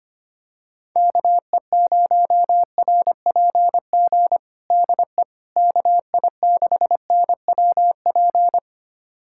Morse code, heard as a keyed tone, about 25 wpm, 700 Hz. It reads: KE0RPG DE KI6NWP
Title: USA Callsign DE Callsign - 6 Characters - 25wpm